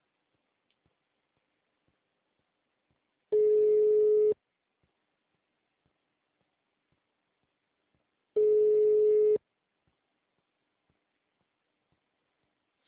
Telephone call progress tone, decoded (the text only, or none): ringback tone